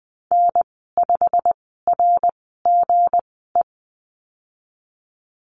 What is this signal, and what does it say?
Morse code, heard as a keyed tone, about 20 wpm, spaced as for 15 wpm, 700 Hz. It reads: N5RGE
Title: 161 Lesson 26g - Letter g - Call Signs 20wpm